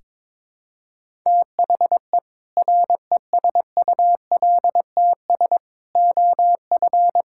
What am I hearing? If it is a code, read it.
Morse code, heard as a keyed tone, about 22 wpm, 700 Hz. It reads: THE RESULTS OF
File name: practice / 22wpm / Build A Phrase - 5-Word Phrases - 22wpm